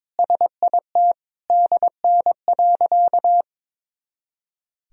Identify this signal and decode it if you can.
Morse code, heard as a keyed tone, about 21 wpm, 700 Hz. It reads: SIT DN.
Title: Sentences from Top 500 Words and Abbreviations 22wpm